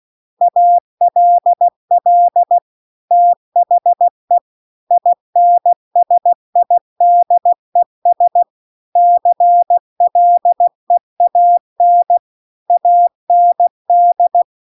Morse code, heard as a keyed tone, about 16 wpm, 700 Hz. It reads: ALL THE INSIDES CLEAN AND